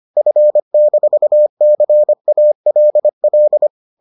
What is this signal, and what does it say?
Morse code, heard as a keyed tone, about 25 wpm, 600 Hz. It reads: F-CALL